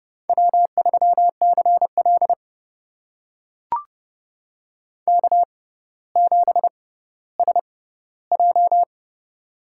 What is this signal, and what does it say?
Morse code, heard as a keyed tone, about 32 wpm, spaced as for 16 wpm, 690 Hz. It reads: W3CL K7SJ
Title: USA Call Signs - 4 Characters - 6x Character Spacing - 30wpm